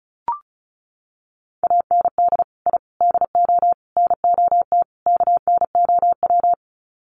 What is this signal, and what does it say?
Morse code, heard as a keyed tone, about 34 wpm, 700 Hz. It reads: AND I DO NOT KNOW
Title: Sets of 5 Words 35wpm